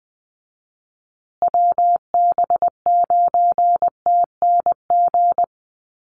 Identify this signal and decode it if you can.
Morse code, heard as a keyed tone, about 20 wpm, 700 Hz. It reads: WB9TNG
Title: USA Call Signs Repeated 3x 20wpm